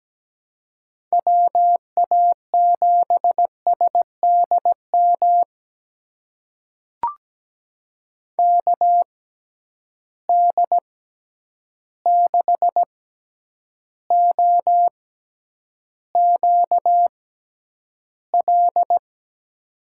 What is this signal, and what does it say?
Morse code, heard as a keyed tone, about 17 wpm, spaced as for 9 wpm, 700 Hz. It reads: WA7SDM KD6OQL
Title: USA Call Signs - 6 Characters - 6x Character Spacing - 17wpm